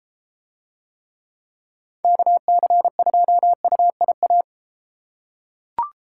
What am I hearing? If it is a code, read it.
Morse code, heard as a keyed tone, about 32 wpm, 700 Hz. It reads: KC2UIA